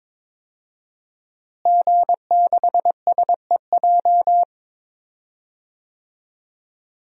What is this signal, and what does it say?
Morse code, heard as a keyed tone, about 22 wpm, 700 Hz. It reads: G6SEJ
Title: DXCC-100 Call Signs - 5 Characters - Repeated 2x - 22wpm